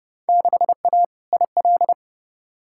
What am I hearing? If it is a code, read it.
Morse code, heard as a keyed tone, about 30 wpm, 700 Hz. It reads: 6A IL
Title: Field Day Exchanges - 30wpm